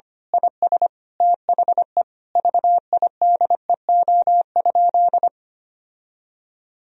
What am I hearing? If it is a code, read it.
Morse code, heard as a keyed tone, about 25 wpm, 700 Hz. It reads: IS THE VIDEO?
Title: Sentences from Top 1000 Words - Review Part 4 - 25wpm